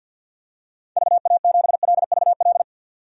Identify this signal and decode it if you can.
Morse code, heard as a keyed tone, about 51 wpm, 700 Hz. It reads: VA7LVL
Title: DXCC-100 Call Signs - 6 Characters - 50wpm